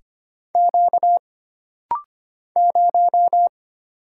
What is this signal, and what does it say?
Morse code, heard as a keyed tone, about 25 wpm, 700 Hz. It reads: Q 0
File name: Single Character (Letter:Number:Punctuation:Prosign) - Rapid-Fire - 25wpm